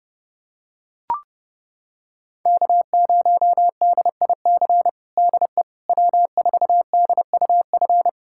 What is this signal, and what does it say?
Morse code, heard as a keyed tone, about 30 wpm, 700 Hz. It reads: K0DIC DE W4DUF
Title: USA Callsign DE Callsign - 5 Characters - 30wpm